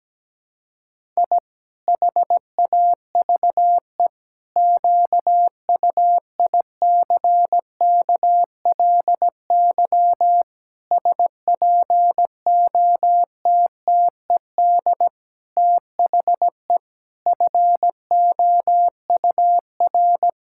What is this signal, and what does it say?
Morse code, heard as a keyed tone, about 17 wpm, 700 Hz. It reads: I HAVE QUICKLY SPOTTED THE FOUR
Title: Pangram Challenge v2 - 17wpm